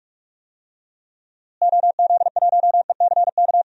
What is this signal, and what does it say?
Morse code, heard as a keyed tone, about 46 wpm, 700 Hz. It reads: OZ1ECK
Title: DXCC-100 Call Signs - 6 Characters - Repeated 3x - 45wpm